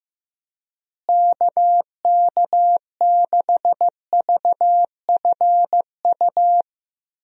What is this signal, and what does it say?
Morse code, heard as a keyed tone, about 15 wpm, 700 Hz. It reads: KK6VFU
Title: USA Call Signs - 6 Characters - Repeated 3x - 15wpm